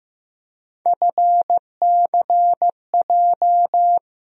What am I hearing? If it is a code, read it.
Morse code, heard as a keyed tone, about 15 wpm, 700 Hz. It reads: FCJ
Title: USA Call Sign Postfixes - 3 Letters 15wpm